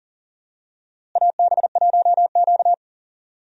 Morse code, heard as a keyed tone, about 40 wpm, 700 Hz. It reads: AB1Q
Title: USA Call Signs - 4 Characters - Repeated 3x - 40wpm